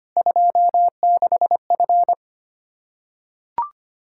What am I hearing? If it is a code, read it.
Morse code, heard as a keyed tone, about 25 wpm, 700 Hz. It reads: 26F
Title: Practice QSO - Exchange 2 - Info - 25wpm